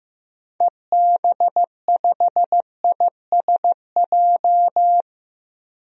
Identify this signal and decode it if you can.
Morse code, heard as a keyed tone, about 16 wpm, 700 Hz. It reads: EB5ISJ
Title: DXCC-100 Call Signs - 6 Characters - 15wpm